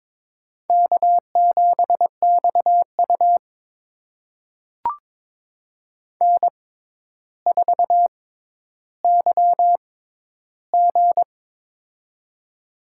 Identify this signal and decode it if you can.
Morse code, heard as a keyed tone, about 22 wpm, spaced as for 11 wpm, 700 Hz. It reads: K7XU N4YG
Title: USA Call Signs - 4 Characters - 6x Character Spacing - 22wpm